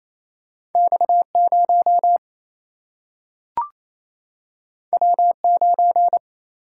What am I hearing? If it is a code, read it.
Morse code, heard as a keyed tone, about 28 wpm, 700 Hz. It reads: X0 W9
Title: Single-Letter Single-Number 28wpm